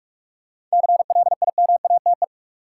Morse code, heard as a keyed tone, about 46 wpm, 700 Hz. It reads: CLIMATE
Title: Seven Letter Words - 45wpm